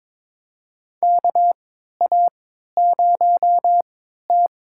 Morse code, heard as a keyed tone, about 22 wpm, spaced as for 13 wpm, 700 Hz. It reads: KA0T
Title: USA Call Signs - 3x Character Spacing - 22wpm